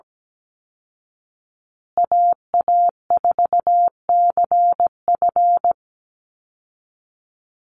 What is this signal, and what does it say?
Morse code, heard as a keyed tone, about 17 wpm, 700 Hz. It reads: AA4CF
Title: Call Signs Repeated 2x - 17wpm